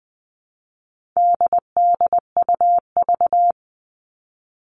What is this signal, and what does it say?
Morse code, heard as a keyed tone, about 20 wpm, 700 Hz. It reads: DDUV
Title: All Cut Numbers - 4-Digit 20wpm